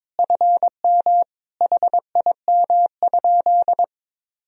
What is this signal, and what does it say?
Morse code, heard as a keyed tone, about 22 wpm, 700 Hz. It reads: FM HIM?
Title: Sentences from Top 300 Words and Abbreviations 22wpm